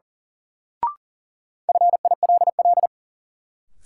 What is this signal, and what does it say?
Morse code, heard as a keyed tone, about 37 wpm, 700 Hz. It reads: FILL
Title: Sentences from Top 1000 Words - Review Part 3 - New Word Drill - 40wpm